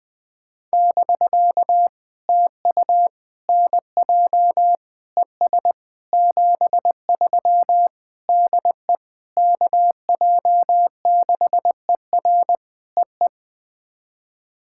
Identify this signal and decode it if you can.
Morse code, heard as a keyed tone, about 18 wpm, 700 Hz. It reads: <BK> TU NJ ES 73 DE KJ6ER EE
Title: POTA QSO Practice - 20wpm